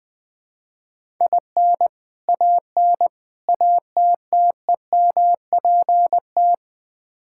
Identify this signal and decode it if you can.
Morse code, heard as a keyed tone, about 20 wpm, 700 Hz. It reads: IN AN ATTEMPT
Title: Build A Phrase - 4-Word Phrases - 20wpm